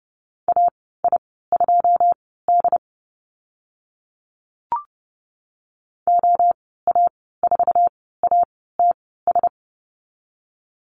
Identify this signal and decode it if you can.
Morse code, heard as a keyed tone, about 30 wpm, spaced as for 17 wpm, 700 Hz. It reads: AI2D OA4ATS
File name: Lesson 14g - Number 2 - Call Signs 30wpm